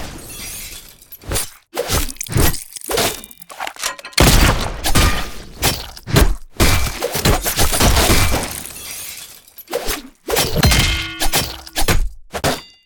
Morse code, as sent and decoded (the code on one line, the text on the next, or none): none